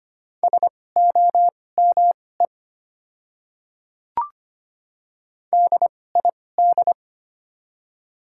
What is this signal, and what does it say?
Morse code, heard as a keyed tone, about 25 wpm, spaced as for 18 wpm, 700 Hz. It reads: SOME DID
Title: 2x Character Spacing - Top 100 Words - 25wpm